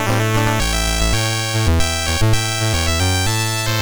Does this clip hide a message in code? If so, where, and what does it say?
No.